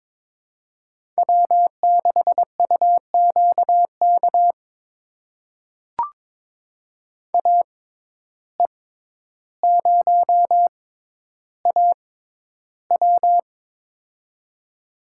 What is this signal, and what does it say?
Morse code, heard as a keyed tone, about 22 wpm, spaced as for 11 wpm, 700 Hz. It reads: W6UQK AE0AW
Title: USA Call Signs - 5 Characters - 6x Character Spacing - 22wpm